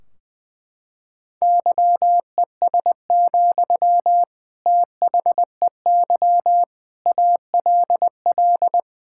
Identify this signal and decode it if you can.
Morse code, heard as a keyed tone, about 20 wpm, 700 Hz. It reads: YES, THEY ALL
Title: Sentences from Top 1600 Words 20wpm